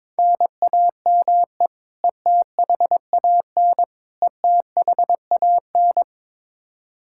Morse code, heard as a keyed tone, about 22 wpm, 700 Hz. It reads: NAME ETHAN ETHAN
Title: Practice QSO - Exchange 1 - Beginning - 22wpm